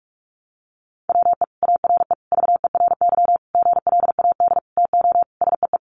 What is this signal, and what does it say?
Morse code, heard as a keyed tone, about 44 wpm, 700 Hz. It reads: WE ARE VERY GLAD TO SEE
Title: Sentences from Top 800 Words 45wpm